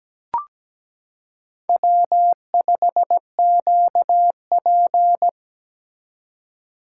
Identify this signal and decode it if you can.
Morse code, heard as a keyed tone, about 17 wpm, 700 Hz. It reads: W5QP
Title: K1USN SST QSO Practice - 17wpm